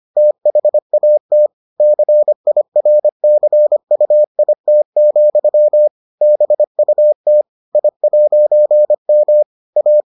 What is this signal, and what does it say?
Morse code, heard as a keyed tone, about 25 wpm, 600 Hz. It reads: THAT CIRCUIT, BUT I'M A